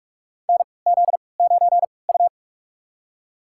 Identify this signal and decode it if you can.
Morse code, heard as a keyed tone, about 42 wpm, spaced as for 25 wpm, 700 Hz. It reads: NZ9U
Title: USA Call Signs - 4 Characters - 3x Character Spacing - 45wpm